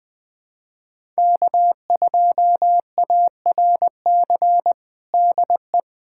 Morse code, heard as a keyed tone, about 20 wpm, 700 Hz. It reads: K2ARC DE
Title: USA Callsign DE Callsign - 5 Characters - 20wpm